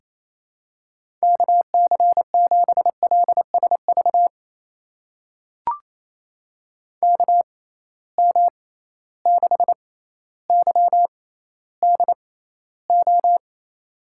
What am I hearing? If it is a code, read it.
Morse code, heard as a keyed tone, about 28 wpm, spaced as for 14 wpm, 700 Hz. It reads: KC7LSV KM6YDO
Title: USA Call Signs - 6 Characters - 6x Character Spacing - 28wpm